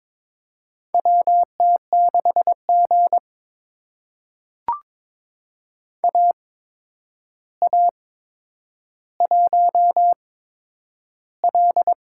Morse code, heard as a keyed tone, about 22 wpm, spaced as for 9 wpm, 700 Hz. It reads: WT6G AA1L
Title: USA Call Signs - 8x Character Spacing - 22wpm